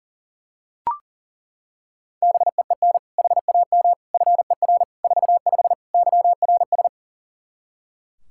Morse code, heard as a keyed tone, about 40 wpm, 700 Hz. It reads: BEEN HAM FER 45 YRS
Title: Practice QSO - Exchange 3 - More Info - 40wpm